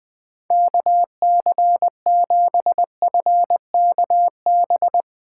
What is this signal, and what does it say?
Morse code, heard as a keyed tone, about 20 wpm, 700 Hz. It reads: KC7FKB
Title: Speed Racing - USA Call Signs 17wpm